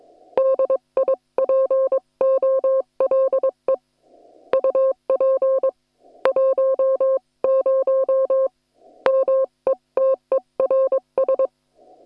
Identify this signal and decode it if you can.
Morse code, heard as a keyed tone, about 22 wpm, 530 Hz. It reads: DIPOLE UP 10 METERS
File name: attachments / hf 1 001